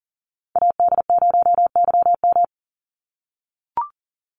Morse code, heard as a keyed tone, about 40 wpm, 700 Hz. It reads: AD0YM
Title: NA SOTA Chaser Callsigns 40wpm